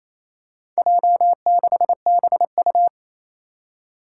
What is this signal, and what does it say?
Morse code, heard as a keyed tone, about 28 wpm, 700 Hz. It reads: J6BU
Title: Four Challenging Letter-Number Combinations 28wpm